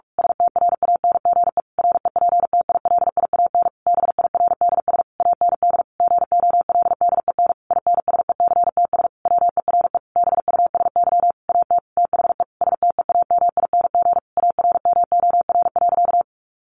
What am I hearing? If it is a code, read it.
Morse code, heard as a keyed tone, about 44 wpm, 700 Hz. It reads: STRANGE REPTILIAN BIRDS AND GOLDEN INSECTS WERE BUSY AT THE STEAMING ARMOR.
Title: A Sound Of Thunder 45wpm